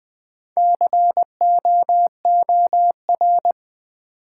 Morse code, heard as a keyed tone, about 20 wpm, 700 Hz. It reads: COOR
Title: Build A Word - 4 Letters At A Time - 10-Letter Words - 20wpm